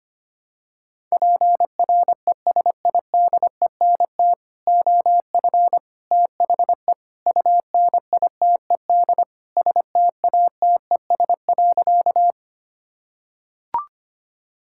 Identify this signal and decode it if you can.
Morse code, heard as a keyed tone, about 25 wpm, 700 Hz. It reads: PRESIDENT OF THE UNITED STATES.
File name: practice / 25wpm / Sentences with 4-Word Phrases - 25wpm